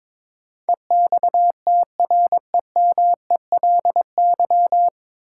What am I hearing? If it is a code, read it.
Morse code, heard as a keyed tone, about 22 wpm, 700 Hz. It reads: EXTREMELY
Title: Sentences from Top 1000 Words - Review Part 7 - New Word Drill - 22wpm